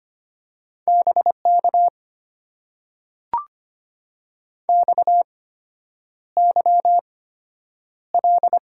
Morse code, heard as a keyed tone, about 25 wpm, spaced as for 9 wpm, 700 Hz. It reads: BK XYL
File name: Top 90 QSO Elements - 8x Character Spacing - 25wpm